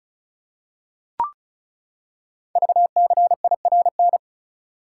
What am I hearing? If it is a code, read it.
Morse code, heard as a keyed tone, about 36 wpm, 700 Hz. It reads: VCIRN
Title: Code Groups - 5 Letter:Numbers - 35wpm